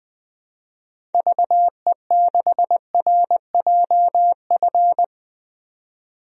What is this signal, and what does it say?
Morse code, heard as a keyed tone, about 20 wpm, 700 Hz. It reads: VE6RJF
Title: DXCC-100 Call Signs - 6 Characters - 20wpm